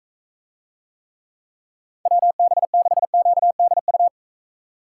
Morse code, heard as a keyed tone, about 43 wpm, 700 Hz. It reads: WB6QDU